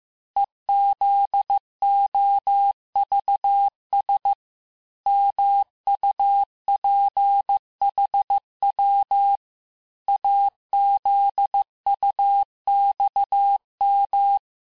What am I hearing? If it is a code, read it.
Morse code, heard as a keyed tone, about 15 wpm, 800 Hz. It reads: EZOVS MUPHW AZUXM
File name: cours-du-lundi files / cours-1585-texte-chiffre-vitesse-14mpm